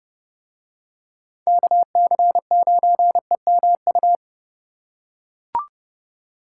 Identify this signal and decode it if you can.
Morse code, heard as a keyed tone, about 29 wpm, 700 Hz. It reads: KC9EMU